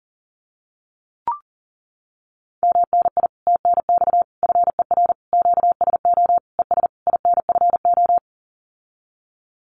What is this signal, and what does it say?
Morse code, heard as a keyed tone, about 40 wpm, 700 Hz. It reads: MNI TNX FER QSO ES INFO
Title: Common QSO Phrases Encore 40wpm